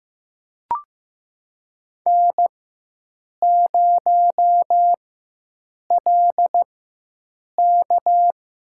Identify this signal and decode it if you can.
Morse code, heard as a keyed tone, about 15 wpm, spaced as for 7 wpm, 700 Hz. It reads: N0LK
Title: USA Call Signs - 4x Character Spacing - 15wpm